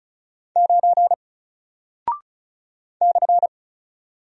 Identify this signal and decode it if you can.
Morse code, heard as a keyed tone, about 36 wpm, 700 Hz. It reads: 9 /
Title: Single Character (Letter:Number:Punctuation:Prosign) Alt 35wpm